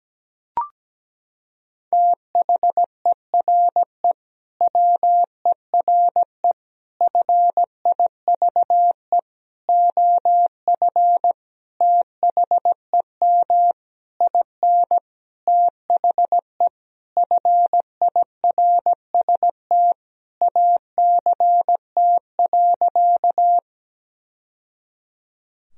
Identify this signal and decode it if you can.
Morse code, heard as a keyed tone, about 17 wpm, 700 Hz. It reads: THERE WERE FIVE OF THEM IN THE FIRST ACT.
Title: Sentences from Top 1000 Words - Review Part 2 - 17wpm